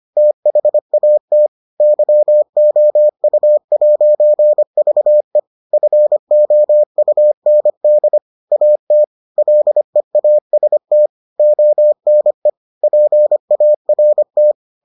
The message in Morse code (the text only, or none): THAT YOU'VE FOUND AT LEAST ONE PART